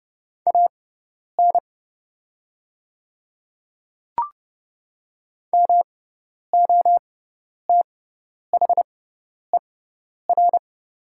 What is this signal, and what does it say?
Morse code, heard as a keyed tone, about 30 wpm, spaced as for 10 wpm, 690 Hz. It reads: AN MOTHER
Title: 6x Character Spacing - Top 200 Words - 30wpm